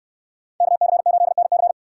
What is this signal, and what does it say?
Morse code, heard as a keyed tone, about 66 wpm, 700 Hz. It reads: KC8NC